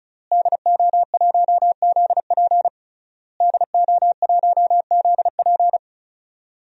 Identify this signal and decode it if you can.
Morse code, heard as a keyed tone, about 36 wpm, 700 Hz. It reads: DO1ZP DO1ZP
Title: DXCC-100 Call Signs - 5 Characters - Repeated 3x - 35wpm